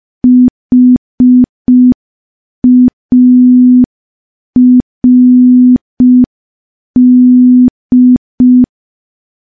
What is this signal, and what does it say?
Morse code, heard as a keyed tone, about 5 wpm, 250 Hz. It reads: HARD
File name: morse / morse-training-05WPM